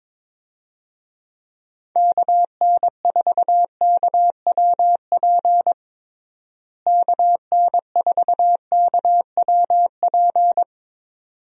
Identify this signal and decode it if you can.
Morse code, heard as a keyed tone, about 22 wpm, 700 Hz. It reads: KN4KWP KN4KWP
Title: USA Call Signs - 6 Characters - Repeated 2x - 22wpm